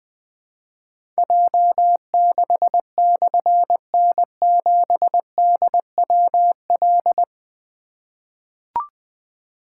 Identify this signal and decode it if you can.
Morse code, heard as a keyed tone, about 20 wpm, 700 Hz. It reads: J6/N7DWL
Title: Difficult Call Signs 20wpm